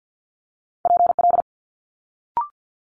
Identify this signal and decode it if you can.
Morse code, heard as a keyed tone, about 50 wpm, 700 Hz. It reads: PL
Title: Multi-letter Phonemes Quick Fire 50wpm